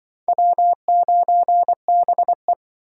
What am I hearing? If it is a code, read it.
Morse code, heard as a keyed tone, about 24 wpm, 700 Hz. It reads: W9BE
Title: Speed Racing - USA 1x2 Call Signs 20wpm